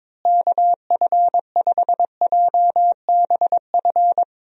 Morse code, heard as a keyed tone, about 21 wpm, 700 Hz. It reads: KF5JBF
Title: USA Call Signs - 6 Characters - Repeated 2x - 22wpm